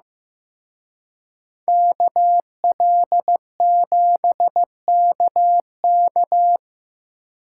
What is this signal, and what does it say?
Morse code, heard as a keyed tone, about 18 wpm, 700 Hz. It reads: KL7KK
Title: Speed Racing - USA Call Signs 15wpm